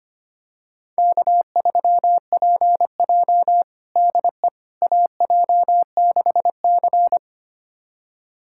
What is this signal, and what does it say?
Morse code, heard as a keyed tone, about 25 wpm, 700 Hz. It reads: K3PJ DE AJ6C
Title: USA Callsign DE Callsign - 4 Characters - 25wpm